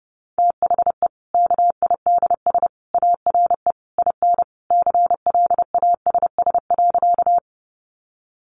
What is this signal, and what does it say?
Morse code, heard as a keyed tone, about 30 wpm, 700 Hz. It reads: THE KIDS ARE IN CLASS.
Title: Easy Phrases 30wpm